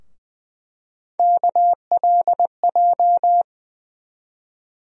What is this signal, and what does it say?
Morse code, heard as a keyed tone, about 20 wpm, 700 Hz. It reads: KLJ
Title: USA Call Sign Postfixes - 3 Letters 20wpm